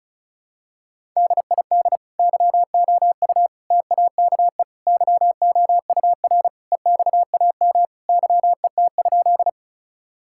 Morse code, heard as a keyed tone, about 36 wpm, 700 Hz. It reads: DID YOU TAKE YOUR EXAM YET?
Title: Sentences from Top 1000 Words - Review Part 5 - 35wpm